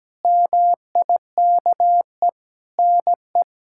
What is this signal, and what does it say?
Morse code, heard as a keyed tone, about 17 wpm, 700 Hz. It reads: MIKE NE
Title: K1USN SST QSO Practice - 17wpm